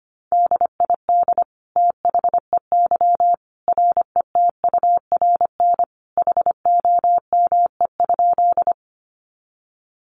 Morse code, heard as a keyed tone, about 25 wpm, 700 Hz. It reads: DID THEY RETURN HOME?
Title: Sentences from Top 800 Words 25wpm